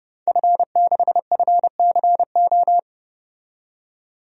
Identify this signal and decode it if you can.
Morse code, heard as a keyed tone, about 32 wpm, 700 Hz. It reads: F6FCO
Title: DXCC-100 Call Signs - 5 Characters - 30wpm